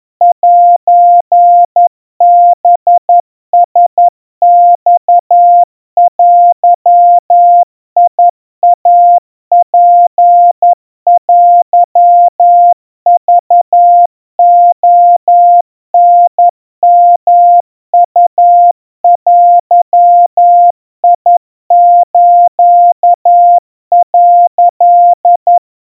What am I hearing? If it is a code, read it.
Morse code (Japanese Wabun code), heard as a keyed tone, about 11 wpm, 700 Hz. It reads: セハラマデイツテクレタヨウデス」
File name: kawa / Morse sample / kamyu 40 cpm 700 Hz Crispy